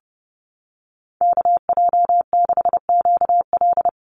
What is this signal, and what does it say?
Morse code, heard as a keyed tone, about 30 wpm, 700 Hz. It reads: KJ6QL
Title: USA Call Signs Repeated 3x 30wpm